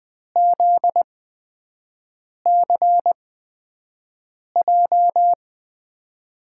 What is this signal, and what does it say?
Morse code, heard as a keyed tone, about 20 wpm, spaced as for 5 wpm, 700 Hz. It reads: ZCJ
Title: USA Call Signs - 6 Characters - 8x Character Spacing - 20wpm